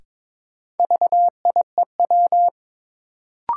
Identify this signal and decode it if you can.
Morse code, heard as a keyed tone, about 22 wpm, 700 Hz. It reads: VIEW
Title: Sets of 5 Words - New Word Drill - 22wpm